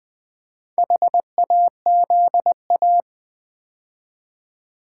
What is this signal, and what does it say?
Morse code, heard as a keyed tone, about 20 wpm, 690 Hz. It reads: HAZA
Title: Build A Word - 2 Letters At A Time - 6-Letter Words - 20wpm